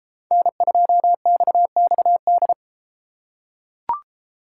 Morse code, heard as a keyed tone, about 33 wpm, 700 Hz. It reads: N2XXD